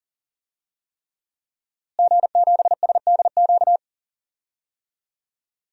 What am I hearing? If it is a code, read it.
Morse code, heard as a keyed tone, about 40 wpm, 700 Hz. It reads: G7SDQ
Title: DXCC-100 Call Signs - 5 Characters - Repeated 3x - 40wpm